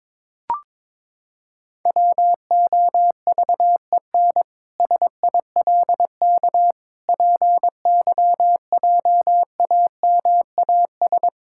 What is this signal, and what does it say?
Morse code, heard as a keyed tone, about 22 wpm, 700 Hz. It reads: WOVEN SILK PYJAMAS
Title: Pangram Challenge v2 - 22wpm